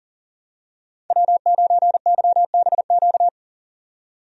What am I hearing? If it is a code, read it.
Morse code, heard as a keyed tone, about 40 wpm, 700 Hz. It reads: W9YBQ
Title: USA Call Signs - 5 Characters - 40wpm